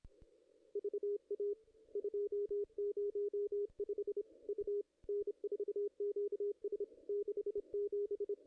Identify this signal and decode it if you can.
Morse code, heard as a keyed tone, about 27 wpm, 410 Hz. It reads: VA 205 U N4QS 67
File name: contests / ss-cw-k5tr-op-n5rz -2300